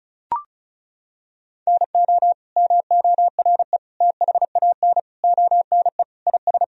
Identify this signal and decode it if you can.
Morse code, heard as a keyed tone, about 34 wpm, 700 Hz. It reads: NO MORE THAN ONE IS
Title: Sentences from Top 1000 Words - Review Part 2 - 35wpm